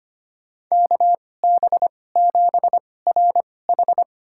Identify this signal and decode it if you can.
Morse code, heard as a keyed tone, about 25 wpm, spaced as for 18 wpm, 700 Hz. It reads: KB7RH
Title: USA Call Signs - 5 Characters - 2x Character Spacing - 25wpm